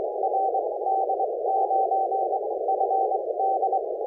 Morse code, heard as a keyed tone, about 23 wpm, spaced as for 18 wpm, 750 Hz, with noise. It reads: RD9FD
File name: 7011cwPileup